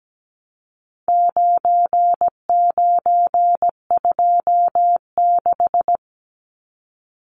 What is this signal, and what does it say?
Morse code, heard as a keyed tone, about 17 wpm, 700 Hz. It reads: 9926
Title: Numbers - 4-Digit 17wpm